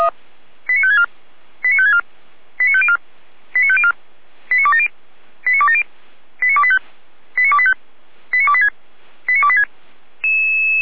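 Five-tone selective call (ZVEI), heard as quick runs of five tones, tones of about 70 ms. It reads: 89564 89564 89504 89504 89270 89270 89285 89285 89286 89286
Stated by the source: ZVEI five-tone call received on SDR.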